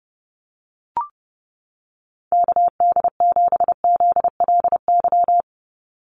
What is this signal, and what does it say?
Morse code, heard as a keyed tone, about 30 wpm, 700 Hz. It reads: KD7ZLY
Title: USA Call Signs 30wpm